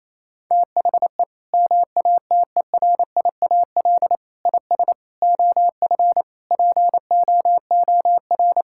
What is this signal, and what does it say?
Morse code, heard as a keyed tone, about 28 wpm, 700 Hz. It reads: THE MATERIAL IS OF POOR
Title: Sentences from Top 1000 Words - Review Part 5 - 28wpm